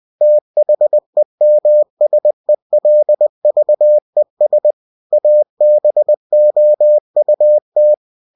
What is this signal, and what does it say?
Morse code, heard as a keyed tone, about 20 wpm, 600 Hz. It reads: THEMSELVES ABOUT
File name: cw-books / winnie-the-pooh / winnie-20wpm-page-0106